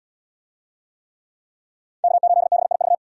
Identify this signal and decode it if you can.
Morse code, heard as a keyed tone, about 77 wpm, 700 Hz. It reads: K8BIV